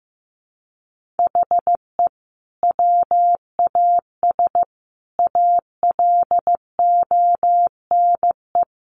Morse code, heard as a keyed tone, about 15 wpm, 700 Hz. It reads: HE WAS ALONE
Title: Sentences from Top 1000 Words 15wpm